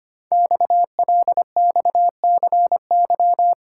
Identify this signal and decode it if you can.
Morse code, heard as a keyed tone, about 25 wpm, 700 Hz. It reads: XLXCY
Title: Code Groups - 5 Letters - 25wpm